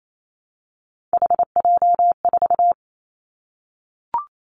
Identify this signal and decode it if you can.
Morse code, heard as a keyed tone, about 28 wpm, 700 Hz. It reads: HJ4
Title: Callsign Prefixes for Top 100 DXCC - 28wpm